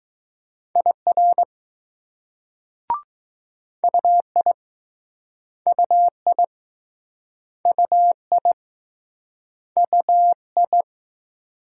Morse code, heard as a keyed tone, about 20 wpm, 700 Hz. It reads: IR UI UI UI UI